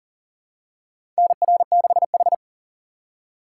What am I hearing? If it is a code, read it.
Morse code, heard as a keyed tone, about 40 wpm, 700 Hz. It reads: NR6H
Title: USA Call Signs - 4x Character Spacing - 40wpm